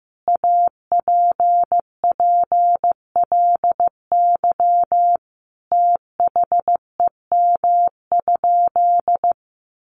Morse code, heard as a keyed tone, about 15 wpm, 700 Hz. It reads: APPLY THEM?
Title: Sentences from Top 1400 Words 15wpm